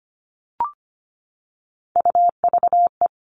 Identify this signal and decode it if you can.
Morse code, heard as a keyed tone, about 25 wpm, 700 Hz. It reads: UVE
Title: All Cut Numbers - 3-Digit 25wpm